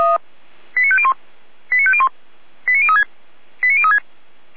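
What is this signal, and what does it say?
Five-tone selective call (ZVEI), heard as quick runs of five tones, tones of about 70 ms. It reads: 89501 89501 89036 89036
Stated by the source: ZVEI five-tone selective call received on an SDR.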